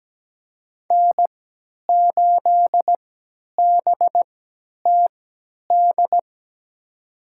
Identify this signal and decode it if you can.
Morse code, heard as a keyed tone, about 17 wpm, spaced as for 10 wpm, 700 Hz. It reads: N8BTD
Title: USA Call Signs - 5 Characters - 3x Character Spacing - 17wpm